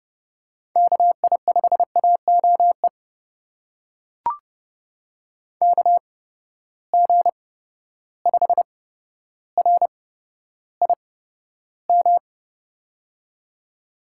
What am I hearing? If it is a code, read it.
Morse code, heard as a keyed tone, about 30 wpm, spaced as for 13 wpm, 700 Hz. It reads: KI5AOE KG5RIM
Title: USA Call Signs - 6 Characters - 8x Character Spacing - 30wpm